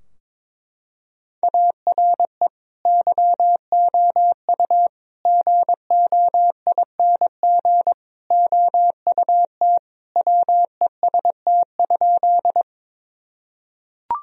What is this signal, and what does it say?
Morse code, heard as a keyed tone, about 22 wpm, 700 Hz. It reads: ARE YOU GOING OUT WEST?
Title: Sentences from Top 1000 Words - Review Part 2 - 22wpm